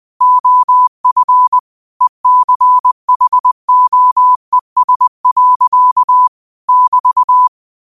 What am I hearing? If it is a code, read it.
Morse code, heard as a keyed tone, about 20 wpm, 1000 Hz. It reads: OF ECHOES. <BT>
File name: qotdfiles / 20wpm